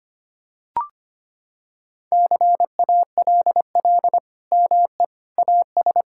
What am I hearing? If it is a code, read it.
Morse code, heard as a keyed tone, about 25 wpm, 700 Hz. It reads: CALL ME AS
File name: practice / 25wpm / Sentences with 3-Word Phrases - 25wpm